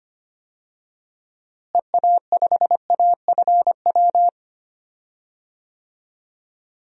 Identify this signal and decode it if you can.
Morse code, heard as a keyed tone, about 25 wpm, 700 Hz. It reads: EA5AFW
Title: DXCC-100 Call Signs - 6 Characters - Repeated 2x - 25wpm